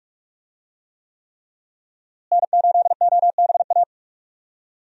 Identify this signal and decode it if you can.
Morse code, heard as a keyed tone, about 46 wpm, 700 Hz. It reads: N8OBA
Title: USA Call Signs - 5 Characters - Repeated 3x - 45wpm